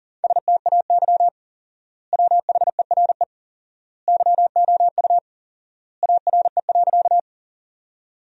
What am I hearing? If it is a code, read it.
Morse code, heard as a keyed tone, about 40 wpm, 700 Hz. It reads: STAY WHERE YOU ARE.
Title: 4x Word-Spacing - Sentences from Top 500 Words - 40wpm